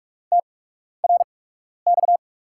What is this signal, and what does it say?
Morse code, heard as a keyed tone, about 47 wpm, spaced as for 12 wpm, 700 Hz. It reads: TRX
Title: USA Call Signs - 8x Character Spacing - 45wpm